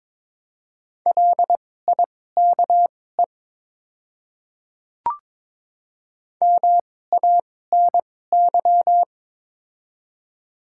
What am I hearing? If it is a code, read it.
Morse code, heard as a keyed tone, about 22 wpm, spaced as for 16 wpm, 700 Hz. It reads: LIKE MANY
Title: 2x Character Spacing - Top 100 Words - 22wpm